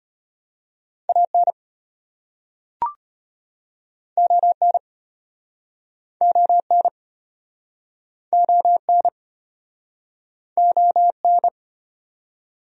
Morse code, heard as a keyed tone, about 34 wpm, 700 Hz. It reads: AN ON ON ON ON